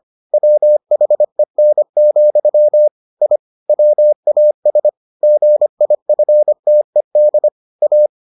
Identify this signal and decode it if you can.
Morse code, heard as a keyed tone, about 25 wpm, 600 Hz. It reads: WHEN, I WAS GIFTED A